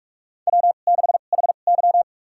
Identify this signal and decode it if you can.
Morse code, heard as a keyed tone, about 46 wpm, spaced as for 33 wpm, 700 Hz. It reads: W6HY
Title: USA Call Signs - 4 Characters - 2x Character Spacing - 45wpm